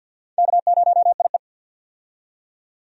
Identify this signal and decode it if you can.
Morse code, heard as a keyed tone, about 51 wpm, 700 Hz. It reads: K0IE
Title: USA Call Signs - 4 Characters - 50wpm